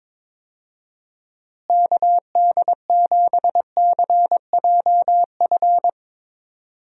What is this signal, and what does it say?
Morse code, heard as a keyed tone, about 22 wpm, 700 Hz. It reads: KD7CJF
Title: USA Call Signs - 6 Characters - Repeated 3x - 22wpm